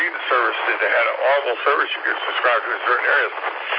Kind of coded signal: Emergency Alert System (SAME) tones